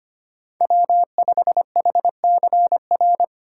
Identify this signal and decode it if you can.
Morse code, heard as a keyed tone, about 25 wpm, 700 Hz. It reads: W5HCR
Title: USA Call Signs - 5 Characters - 2x Character Spacing - 25wpm